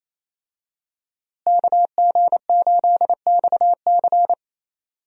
Morse code, heard as a keyed tone, about 28 wpm, 700 Hz. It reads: KG8XC
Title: USA Call Signs - 5 Characters - Repeated 3x - 28wpm